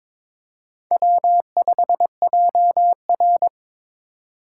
Morse code, heard as a keyed tone, about 22 wpm, 700 Hz. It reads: W5JR
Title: USA Call Signs - 4 Characters - 2x Character Spacing - 22wpm